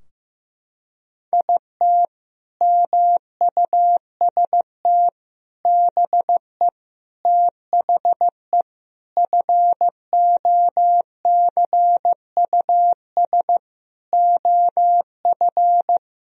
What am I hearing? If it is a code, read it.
Morse code, heard as a keyed tone, about 15 wpm, 700 Hz. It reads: IT MUST BE THE FOCUS OF
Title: Sentences from Top 1000 Words - Review Part 4 - 15wpm